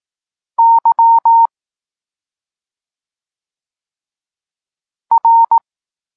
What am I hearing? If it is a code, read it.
Morse code, heard as a keyed tone, about 18 wpm, 920 Hz. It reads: Y R
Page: Tekst 32 – Doskonalenie odbioru poznanych liter